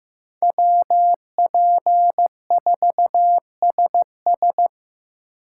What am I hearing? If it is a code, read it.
Morse code, heard as a keyed tone, about 15 wpm, 700 Hz. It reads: WP4SS
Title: USA Call Signs - 5 Characters - 3x Character Spacing - 15wpm